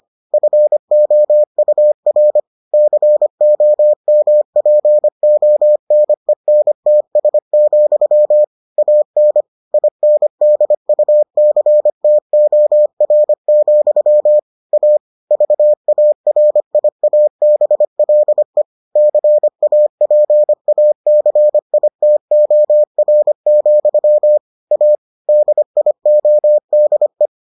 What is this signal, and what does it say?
Morse code, heard as a keyed tone, about 25 wpm, 600 Hz. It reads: FOUR COMPONENTS, AN INDUCTOR, A VARIABLE CAPACITOR, A DIODE